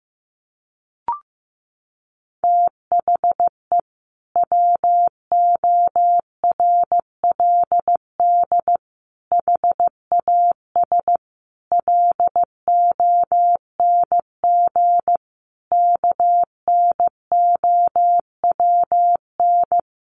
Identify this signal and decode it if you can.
Morse code, heard as a keyed tone, about 15 wpm, 700 Hz. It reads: THE WORLD HAS LONG KNOWN
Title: Sentences from Top 1200 Words 15wpm